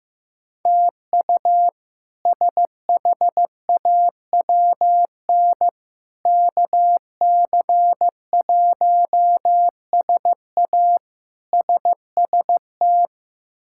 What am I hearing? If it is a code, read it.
Morse code, heard as a keyed tone, about 15 wpm, 700 Hz. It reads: TU SHAWN KC1SA SST
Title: K1USN SST QSO Practice - 15wpm